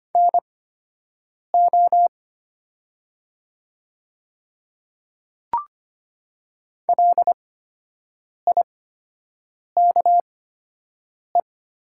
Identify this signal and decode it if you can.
Morse code, heard as a keyed tone, about 25 wpm, spaced as for 7 wpm, 700 Hz. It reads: NO LIKE
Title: 8x Character Spacing - Top 100 Words - 25wpm